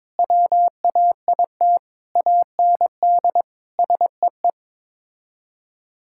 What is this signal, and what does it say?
Morse code, heard as a keyed tone, about 21 wpm, 700 Hz. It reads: WAIT AND SEE
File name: practice / 22wpm / Sets of 7 Words - 22wpm